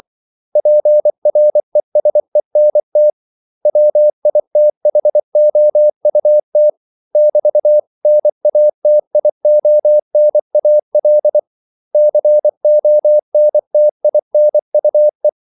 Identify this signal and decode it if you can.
Morse code, heard as a keyed tone, about 24 wpm, 600 Hz. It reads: PRESENT WITHOUT <BT>NATIONAL CONTINUE
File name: CW / Practice / 500MostCommonEnglishWords24